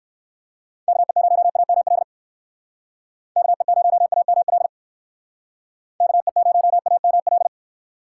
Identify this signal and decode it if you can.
Morse code, heard as a keyed tone, about 62 wpm, 700 Hz. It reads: KE0AML KE0AML KE0AML